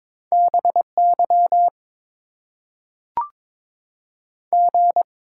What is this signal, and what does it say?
Morse code, heard as a keyed tone, about 22 wpm, 700 Hz. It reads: BY G
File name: Top 90 QSO Elements - 8x Character Spacing - 22wpm